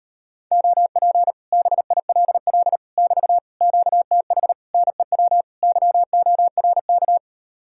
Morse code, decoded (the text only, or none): OP BILL <BT> QTH NEW YORK